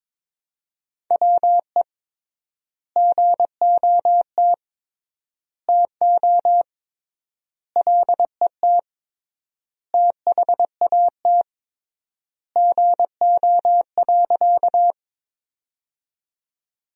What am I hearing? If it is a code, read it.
Morse code, heard as a keyed tone, about 22 wpm, 700 Hz. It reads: WE GOT TO LET THAT GO.
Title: 3x Word-Spacing - Sentences from Top 300 Words - 22wpm